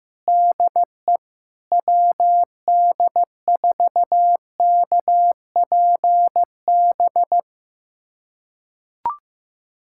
Morse code, heard as a keyed tone, about 15 wpm, 700 Hz. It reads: DE WD4KPB
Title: USA Callsign DE Callsign 15wpm